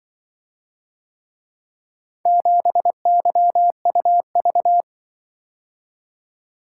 Morse code, heard as a keyed tone, about 24 wpm, 700 Hz. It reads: 7YUV
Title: Speed-Racing - Four Letter-Number Combinations - 20wpm